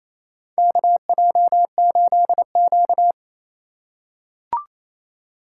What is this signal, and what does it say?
Morse code, heard as a keyed tone, about 28 wpm, 700 Hz. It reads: KJ8Q
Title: USA Call Signs - 3x Character Spacing - 28wpm